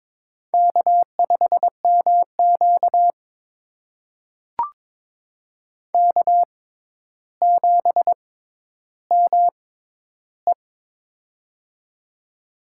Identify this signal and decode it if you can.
Morse code, heard as a keyed tone, about 23 wpm, spaced as for 11 wpm, 700 Hz. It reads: K5MQ K7ME
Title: USA Call Signs - 4 Characters - 6x Character Spacing - 22wpm